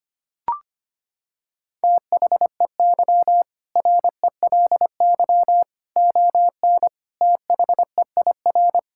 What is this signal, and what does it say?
Morse code, heard as a keyed tone, about 25 wpm, 700 Hz. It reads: THEY RELY ON THEIR
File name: practice / 25wpm / Sentences with 2-Word Phrases - 25wpm